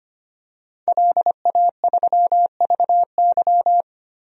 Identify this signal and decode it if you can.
Morse code, heard as a keyed tone, about 25 wpm, 700 Hz. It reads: LA3VY
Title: DXCC-100 Call Signs - 5 Characters - 25wpm